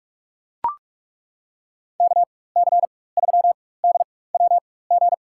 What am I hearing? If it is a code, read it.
Morse code, heard as a keyed tone, about 42 wpm, spaced as for 21 wpm, 700 Hz. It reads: KC3DWG
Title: USA Call Signs - 4x Character Spacing - 45wpm